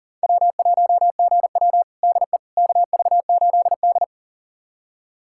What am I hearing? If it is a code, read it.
Morse code, heard as a keyed tone, about 40 wpm, 700 Hz. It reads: W1GW DE KV8D
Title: USA Callsign DE Callsign - 4 Characters - 40wpm